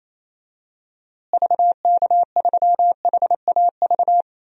Morse code, heard as a keyed tone, about 27 wpm, 700 Hz. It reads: VK3HAV
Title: DXCC-100 Call Signs - 6 Characters - Repeated 2x - 28wpm